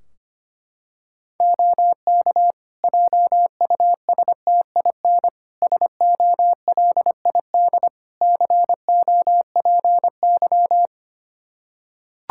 Morse code, heard as a keyed tone, about 25 wpm, 700 Hz. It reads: OK JUSTIN SOLID COPY
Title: Practice QSO - Exchange 3 - More Info - 25wpm